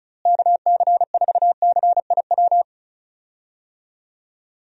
Morse code, heard as a keyed tone, about 34 wpm, 700 Hz. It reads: KC4CIW
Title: USA Call Signs - 6 Characters - Repeated 2x - 35wpm